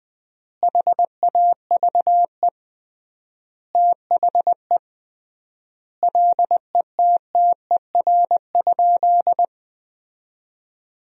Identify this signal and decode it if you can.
Morse code, heard as a keyed tone, about 20 wpm, 700 Hz. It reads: HAVE THE LETTER?
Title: 3x Word-Spacing - Sentences from Top 300 Words - 20wpm